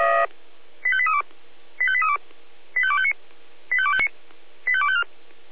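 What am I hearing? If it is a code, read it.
Five-tone selective call (ZVEI), heard as quick runs of five tones, tones of about 70 ms. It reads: 85832 85832 85259 85259 85254
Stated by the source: ZVEI five-tone call 85211 received on SDR, feuerwehr-orlamuende.org fire service.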